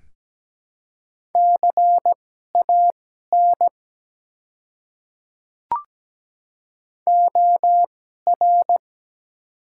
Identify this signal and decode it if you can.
Morse code, heard as a keyed tone, about 17 wpm, spaced as for 12 wpm, 700 Hz. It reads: CAN OR
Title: 2x Character Spacing - Top 100 Words - 17wpm